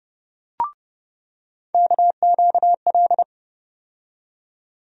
Morse code, heard as a keyed tone, about 30 wpm, 700 Hz. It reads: KQL
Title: USA Call Sign Postfixes - 3 Letters 30wpm